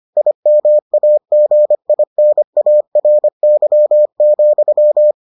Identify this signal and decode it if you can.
Morse code, heard as a keyed tone, about 25 wpm, 600 Hz. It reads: IMAGINARY,